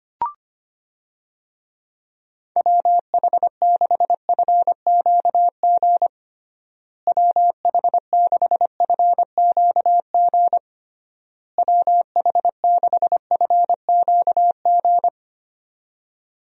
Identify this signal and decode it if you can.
Morse code, heard as a keyed tone, about 25 wpm, 700 Hz. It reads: WH6FQG WH6FQG WH6FQG
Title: USA Call Signs - 6 Characters - Repeated 3x - 25wpm